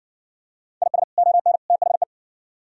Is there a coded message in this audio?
Yes: Morse code from 0.8 to 2.0 s, heard as a keyed tone, about 57 wpm, 700 Hz. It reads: IS ON THE